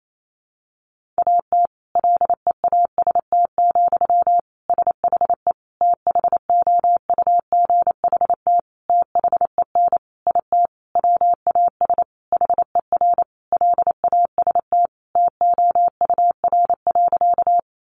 Morse code, heard as a keyed tone, about 28 wpm, 700 Hz. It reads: AT LEAST, SHE THOUGHT THEN IT WAS HER LAST TOUR.
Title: Sentences from Top 1400 Words 28wpm